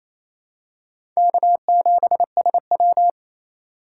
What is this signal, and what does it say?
Morse code, heard as a keyed tone, about 28 wpm, 700 Hz. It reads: K7SW
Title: USA Call Signs - 4x Character Spacing - 28wpm